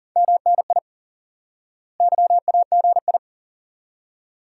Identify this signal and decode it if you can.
Morse code, heard as a keyed tone, about 40 wpm, 700 Hz. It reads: MNI YAGI
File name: IQE Course 3 - Lesson 89 - YAGI - Two Elements - 40wpm